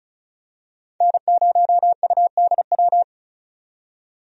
Morse code, heard as a keyed tone, about 34 wpm, 700 Hz. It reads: N0UDW
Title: USA Call Signs - 5 Characters - 35wpm